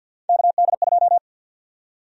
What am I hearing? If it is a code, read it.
Morse code, heard as a keyed tone, about 52 wpm, 700 Hz. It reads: KD2
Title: USA Call Sign Prefixes - 50wpm